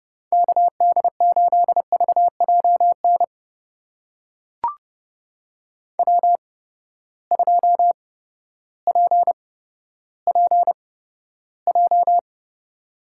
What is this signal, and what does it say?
Morse code, heard as a keyed tone, about 30 wpm, spaced as for 14 wpm, 700 Hz. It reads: KD8VJN W2PPJ
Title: USA Call Signs - 8x Character Spacing - 30wpm